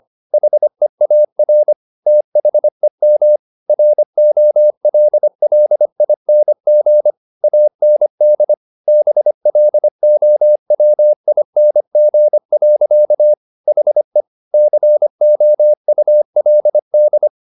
Morse code, heard as a keyed tone, about 25 wpm, 600 Hz. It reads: HEAR THEM ROLLING AND BLOWING. HE COULD
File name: the old man and the sea part 1 25wpm